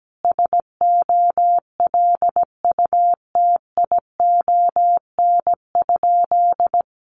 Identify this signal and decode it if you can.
Morse code, heard as a keyed tone, about 17 wpm, 700 Hz. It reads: SOLUTION?
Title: Sentences from Top 1300 Words 17wpm